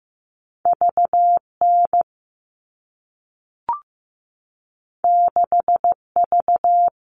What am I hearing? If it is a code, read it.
Morse code, heard as a keyed tone, about 15 wpm, 700 Hz. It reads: VN 6V
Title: All Cut Numbers - 2-Digit 15wpm